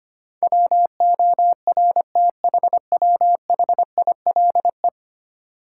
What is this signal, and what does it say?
Morse code, heard as a keyed tone, about 25 wpm, 700 Hz. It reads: WORTHWHILE
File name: Ten Letter Words - 25wpm